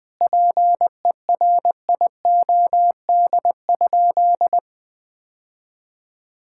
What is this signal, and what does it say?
Morse code, heard as a keyed tone, about 20 wpm, 700 Hz. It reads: PERIOD?
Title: Sentences from Top 1000 Words - Review Part 5 - 20wpm